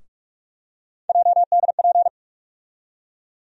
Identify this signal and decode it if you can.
Morse code, heard as a keyed tone, about 46 wpm, 700 Hz. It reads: JDP
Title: USA Call Sign Postfixes - 3 Letters 45wpm